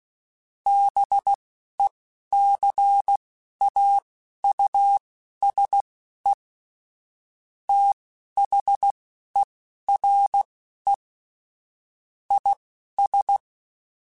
Morse code, heard as a keyed tone, about 16 wpm, spaced as for 11 wpm, 800 Hz. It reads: BECAUSE THERE IS
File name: Technical Subjects / Technical Subjects-mp3s / AND Gate @10